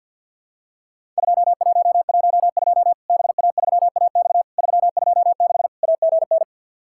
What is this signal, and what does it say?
Morse code, heard as a keyed tone, about 50 wpm, 700 Hz. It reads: 2112 BA3AX 326 AGN
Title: Worked All Europe DX Contest - QTC Exchange with Repeats - 50wpm